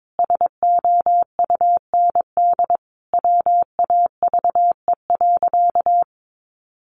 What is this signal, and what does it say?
Morse code, heard as a keyed tone, about 22 wpm, 700 Hz. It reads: SOUND WAVE.
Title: Sentences from Top 1400 Words 22wpm